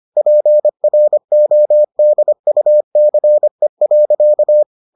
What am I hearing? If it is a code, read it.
Morse code, heard as a keyed tone, about 25 wpm, 600 Hz. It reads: PRODUCE.